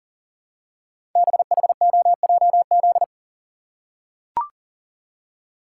Morse code, heard as a keyed tone, about 40 wpm, 700 Hz. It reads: BHOJZ
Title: Code Groups - 5 Letters - 40wpm